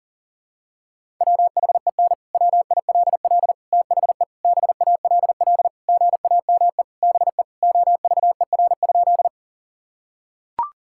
Morse code, heard as a keyed tone, about 37 wpm, 700 Hz. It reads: WHEN WILL THE BALL GAME BE OVER?
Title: Sentences from Top 1000 Words - Review Part 6 - 40wpm